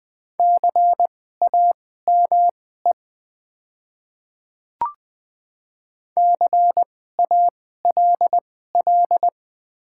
Morse code, heard as a keyed tone, about 20 wpm, spaced as for 14 wpm, 700 Hz. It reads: CAME CALL
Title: 2x Character Spacing - Top 300 Words - 20wpm